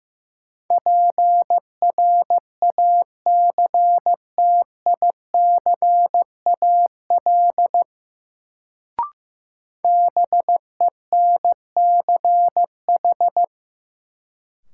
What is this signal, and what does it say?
Morse code, heard as a keyed tone, about 16 wpm, 700 Hz. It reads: PRACTICAL BENCH
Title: Sentences from Top 1400 Words - New Word Drill - 15wpm